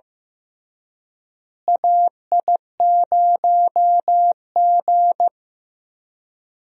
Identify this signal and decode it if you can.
Morse code, heard as a keyed tone, about 15 wpm, 700 Hz. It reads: AI0G
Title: USA Call Signs - 4 Characters - Repeated 3x - 15wpm